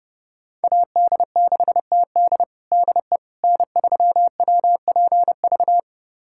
Morse code, heard as a keyed tone, about 31 wpm, 700 Hz. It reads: AD6TD DE N3WPV
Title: USA Callsign DE Callsign - 5 Characters - 30wpm